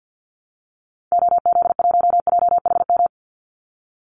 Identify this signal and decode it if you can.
Morse code, heard as a keyed tone, about 48 wpm, 700 Hz. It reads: OZ1JHM
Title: Call Signs - 50wpm